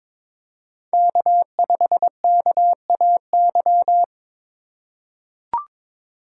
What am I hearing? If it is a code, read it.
Morse code, heard as a keyed tone, about 23 wpm, 700 Hz. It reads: K5KAY
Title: USA Call Signs - 5 Characters - 2x Character Spacing - 22wpm